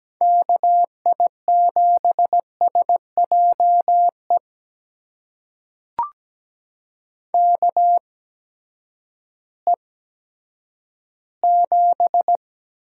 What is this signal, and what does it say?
Morse code, heard as a keyed tone, about 17 wpm, spaced as for 10 wpm, 700 Hz. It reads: KI7SJE KE7
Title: USA Call Signs - 6 Characters - 8x Character Spacing - 17wpm